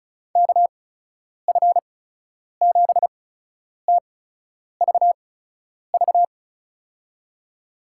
Morse code, heard as a keyed tone, about 34 wpm, spaced as for 10 wpm, 700 Hz. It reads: KF7TVV
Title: USA Call Signs - 6 Characters - 8x Character Spacing - 35wpm